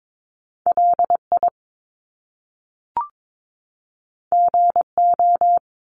Morse code, heard as a keyed tone, about 22 wpm, 700 Hz. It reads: LI GO
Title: Top Two Letter Combinations in Words 22wpm